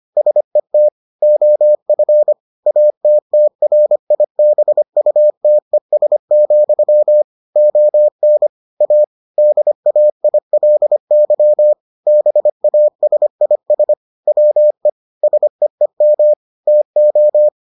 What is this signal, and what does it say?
Morse code, heard as a keyed tone, about 25 wpm, 600 Hz. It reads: SET OF ATTRIBUTES, ON A DAILY BASIS WE SEEM TO